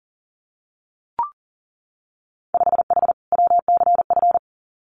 Morse code, heard as a keyed tone, about 40 wpm, 700 Hz. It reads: 5H WCF
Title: Winter Field Day Exchanges Encore 40wpm